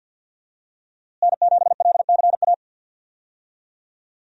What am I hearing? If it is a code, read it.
Morse code, heard as a keyed tone, about 52 wpm, 700 Hz. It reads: N7LCA
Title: USA Call Signs - 5 Characters - Repeated 2x - 50wpm